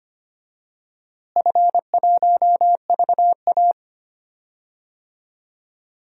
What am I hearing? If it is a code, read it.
Morse code, heard as a keyed tone, about 25 wpm, 700 Hz. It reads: F1VA
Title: DXCC-100 Call Signs - 4 Characters - Repeated 3x - 25wpm